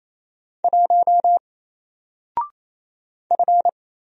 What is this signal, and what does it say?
Morse code, heard as a keyed tone, about 27 wpm, 700 Hz. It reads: 1 F
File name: Single-Letter-Number-28wpm